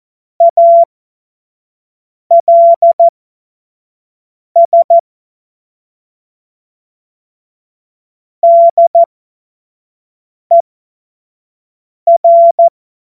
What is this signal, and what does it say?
Morse code, heard as a keyed tone, about 14 wpm, spaced as for 5 wpm, 680 Hz. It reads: ALS DER
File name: Morsekurs / Witze / Witz 01. Juni 2021, 25 F70